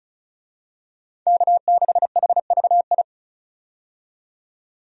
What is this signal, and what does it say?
Morse code, heard as a keyed tone, about 36 wpm, 700 Hz. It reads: K6HVI
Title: USA Call Signs - 5 Characters - 35wpm